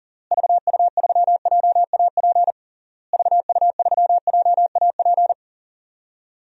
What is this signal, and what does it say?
Morse code, heard as a keyed tone, about 40 wpm, 700 Hz. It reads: VU3JAP VU3JAP
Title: DXCC-100 Call Signs - 6 Characters - Repeated 3x - 40wpm